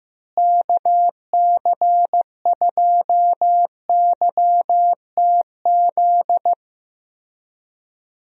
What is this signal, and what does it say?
Morse code, heard as a keyed tone, about 15 wpm, 700 Hz. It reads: KC2YTZ
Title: USA Call Signs - 6 Characters - Repeated 2x - 15wpm